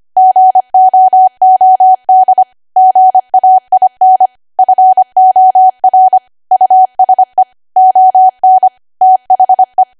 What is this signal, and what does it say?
Morse code, heard as a keyed tone, about 25 wpm, 750 Hz. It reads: GOOD GAIN FOR USE ON THE
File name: Archive / 25 WPM / 25WPM